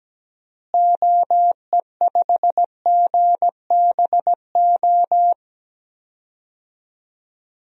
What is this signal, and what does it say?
Morse code, heard as a keyed tone, about 17 wpm, 700 Hz. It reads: OE5GBO
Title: DXCC-100 Call Signs - 6 Characters - Repeated 2x - 17wpm